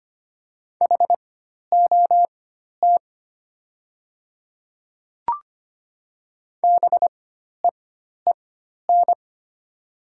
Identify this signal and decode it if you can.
Morse code, heard as a keyed tone, about 25 wpm, spaced as for 12 wpm, 690 Hz. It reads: HOT BEEN
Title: 4x Character Spacing - Top 100 Words - 25wpm